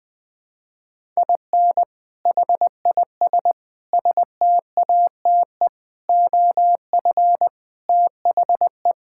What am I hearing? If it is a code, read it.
Morse code, heard as a keyed tone, about 20 wpm, 700 Hz. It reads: IN HIS STATE OF THE
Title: Sentences with 7-Word Phrases - 20wpm